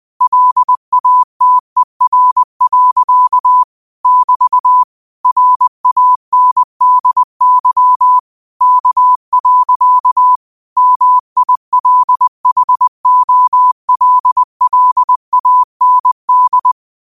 Morse code, heard as a keyed tone, about 20 wpm, 1000 Hz. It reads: LATER. <BT> RANDY K. MILHOLLAND
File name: qotdfiles / 20wpm